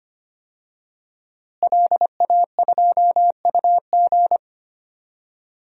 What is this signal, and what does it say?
Morse code, heard as a keyed tone, about 25 wpm, 700 Hz. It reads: LA2UG
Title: DXCC-100 Call Signs - 5 Characters - Repeated 2x - 25wpm